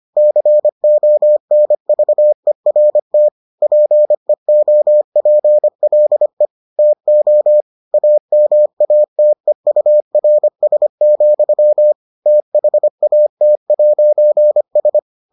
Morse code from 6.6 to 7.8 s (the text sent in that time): TO